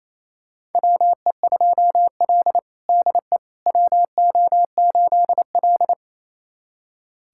Morse code, heard as a keyed tone, about 28 wpm, 700 Hz. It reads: WE2L DE WO8L
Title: USA Callsign DE Callsign - 4 Characters - 28wpm